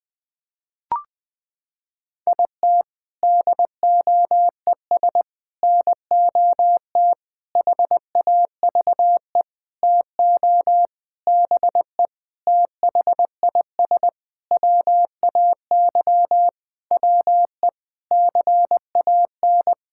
Morse code, heard as a keyed tone, about 20 wpm, 700 Hz. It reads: IT DOES NOT HAVE TO BE THIS WAY WE CAN
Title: Sentences with 7-Word Phrases - 20wpm